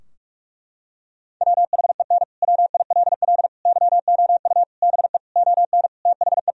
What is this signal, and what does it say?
Morse code, heard as a keyed tone, about 46 wpm, 700 Hz. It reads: WHEN WILL YOU BE ON THE
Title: Sentences from Top 1000 Words - Review Part 2 - 45wpm